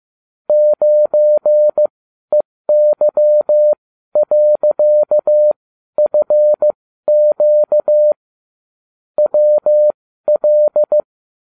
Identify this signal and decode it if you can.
Morse code, heard as a keyed tone, about 15 wpm, spaced as for 12 wpm, 600 Hz. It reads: 9EY.FQ WL